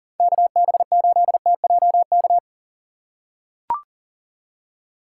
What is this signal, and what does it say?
Morse code, heard as a keyed tone, about 37 wpm, 700 Hz. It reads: KB8TJK
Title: USA Call Signs - 6 Characters - 6x Character Spacing - 40wpm